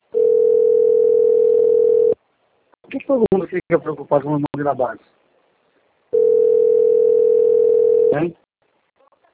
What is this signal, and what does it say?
Ringback tone (North America), 440 + 480 Hz, with speech.
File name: rg-1600-unknown- - - .582